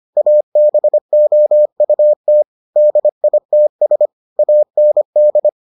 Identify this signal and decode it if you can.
Morse code, heard as a keyed tone, about 25 wpm, 600 Hz. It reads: ABOUT DITS AND